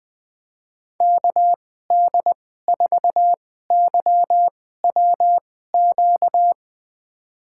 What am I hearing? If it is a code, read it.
Morse code, heard as a keyed tone, about 20 wpm, spaced as for 14 wpm, 700 Hz. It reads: KD4YWQ
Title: USA Call Signs - 6 Characters - 2x Character Spacing - 20wpm